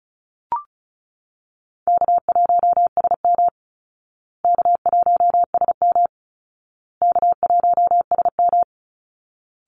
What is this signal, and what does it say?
Morse code, heard as a keyed tone, about 34 wpm, 700 Hz. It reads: K1SM K1SM K1SM
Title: Easy Call Signs Repeated 3x - 35wpm